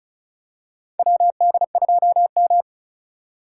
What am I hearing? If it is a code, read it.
Morse code, heard as a keyed tone, about 36 wpm, 700 Hz. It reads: WD2M
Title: USA Call Signs - 4 Characters - 2x Character Spacing - 35wpm